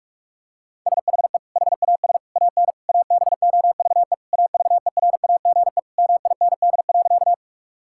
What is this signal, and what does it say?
Morse code, heard as a keyed tone, about 46 wpm, 700 Hz. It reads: SHE HAS AN ABOVE AVERAGE MIND.
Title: Sentences from Top 1000 Words - Review Part 6 - 45wpm